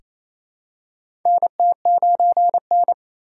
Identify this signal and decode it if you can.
Morse code, heard as a keyed tone, about 28 wpm, 700 Hz. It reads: NT9N
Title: USA Call Signs - 4 Characters - 4x Character Spacing - 28wpm